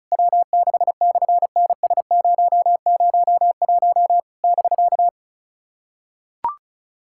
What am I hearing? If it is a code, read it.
Morse code, heard as a keyed tone, about 33 wpm, 700 Hz. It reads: W6/NS001 <BK>
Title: SOTA QSO Practice 35wpm